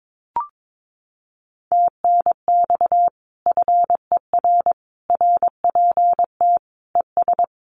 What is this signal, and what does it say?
Morse code, heard as a keyed tone, about 22 wpm, 700 Hz. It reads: TNX FER RPT ES
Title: Common QSO Phrases Encore 22wpm